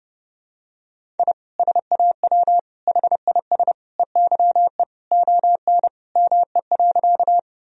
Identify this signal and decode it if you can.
Morse code, heard as a keyed tone, about 32 wpm, 700 Hz. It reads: I SAW HIS EYE ON ME.
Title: Sentences from Top 1000 Words - Review Part 1 - 30wpm